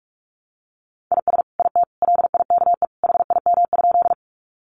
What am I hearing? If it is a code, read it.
Morse code, heard as a keyed tone, about 44 wpm, 700 Hz. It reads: IS IT LIKE HIM?
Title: Sentences from Top 100 Words 45wpm